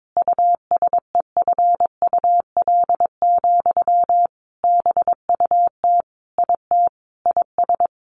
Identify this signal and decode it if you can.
Morse code, heard as a keyed tone, about 22 wpm, 700 Hz. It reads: USEFUL, BUT IT IS
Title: Sentences from Top 1400 Words 22wpm